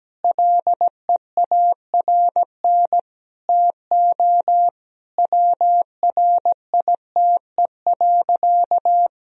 Morse code, heard as a keyed tone, about 17 wpm, 700 Hz. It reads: LEARN TO WRITE.
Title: Sentences from Top 1000 Words - Review Part 1 - 17wpm